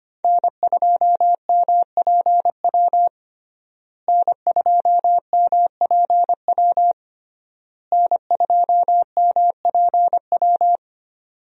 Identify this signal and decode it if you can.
Morse code, heard as a keyed tone, about 25 wpm, 700 Hz. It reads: N2MPW N2MPW N2MPW
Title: USA Call Signs - 5 Characters - Repeated 3x - 25wpm